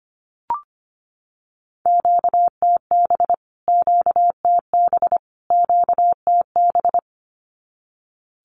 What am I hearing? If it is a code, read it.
Morse code, heard as a keyed tone, about 25 wpm, 700 Hz. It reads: QTB QTB QTB
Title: Q-Codes Repeated 3x 25wpm